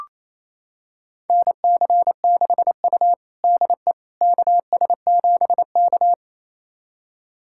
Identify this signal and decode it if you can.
Morse code, heard as a keyed tone, about 28 wpm, 700 Hz. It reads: NC6U DE KS7K
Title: USA Callsign DE Callsign - 4 Characters - 28wpm